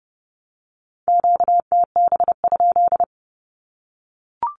Morse code, heard as a keyed tone, about 30 wpm, 700 Hz. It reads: QTB?
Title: Q-Codes Repeated 3x 30wpm